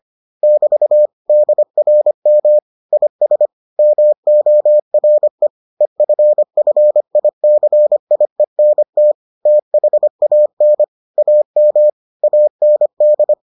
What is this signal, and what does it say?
Morse code, heard as a keyed tone, about 25 wpm, 600 Hz. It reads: <BT> DRM IS MORE EFFICIENT THAN AM AND